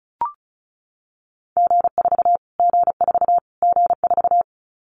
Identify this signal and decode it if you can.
Morse code, heard as a keyed tone, about 34 wpm, 700 Hz. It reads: G4 G4 G4
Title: Callsign Prefixes for Top 25 DXCC - Repeated 3x - 35wpm